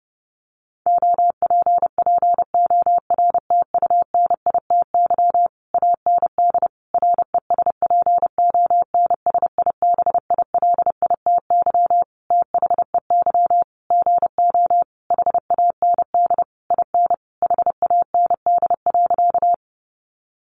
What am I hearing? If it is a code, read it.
Morse code, heard as a keyed tone, about 30 wpm, 700 Hz. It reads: OPPORTUNITY AND RESPONSIBILITY THEY GO HAND IN HAND.
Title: Sentences from Top 1200 Words 30wpm